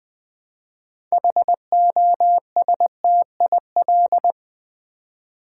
Morse code, heard as a keyed tone, about 20 wpm, 700 Hz. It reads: HOSTIL
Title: Build A Word - 3 Letters At A Time - 9-Letter Words - 20wpm